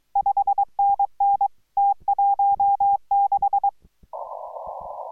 Morse code, heard as a keyed tone, about 24 wpm, 810 Hz. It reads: 5NN T16